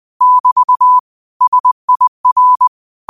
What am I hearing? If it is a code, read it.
Morse code, heard as a keyed tone, about 20 wpm, 1000 Hz. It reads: <BT> SIR
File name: qotdfiles / 20wpm